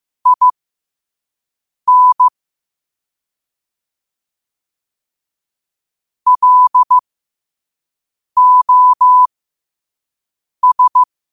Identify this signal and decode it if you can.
Morse code, heard as a keyed tone, about 15 wpm, spaced as for 5 wpm, 1000 Hz. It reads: IN LOS
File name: qotdfiles / 5wpm